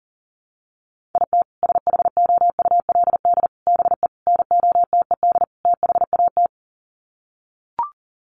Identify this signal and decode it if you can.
Morse code, heard as a keyed tone, about 40 wpm, 700 Hz. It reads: IT SHOULD BE NOTED THAT
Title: Sets of 5 Words 40wpm